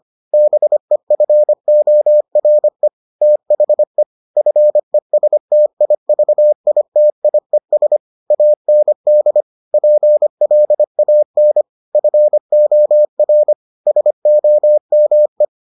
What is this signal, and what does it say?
Morse code, heard as a keyed tone, about 25 wpm, 600 Hz. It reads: BEFORE THE FESTIVITIES AND PLAN FOR SOME